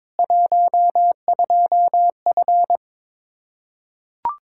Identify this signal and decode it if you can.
Morse code, heard as a keyed tone, about 22 wpm, 700 Hz. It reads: 12F
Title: Practice QSO - Exchange 2 - Info - 22wpm